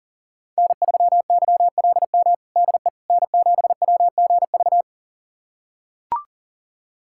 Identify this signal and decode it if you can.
Morse code, heard as a keyed tone, about 37 wpm, 700 Hz. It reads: N3YLM DE N7WGV
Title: USA Callsign DE Callsign - 5 Characters - 40wpm